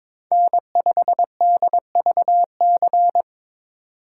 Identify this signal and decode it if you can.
Morse code, heard as a keyed tone, about 22 wpm, 700 Hz. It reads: N5DVC
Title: USA Call Signs - 5 Characters - 8x Character Spacing - 22wpm